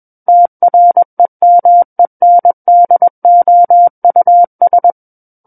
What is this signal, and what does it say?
Morse code, heard as a keyed tone, about 21 wpm, 700 Hz. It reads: TREMENDOUS
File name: Book One / 12-Chapter 12 - 21 WPM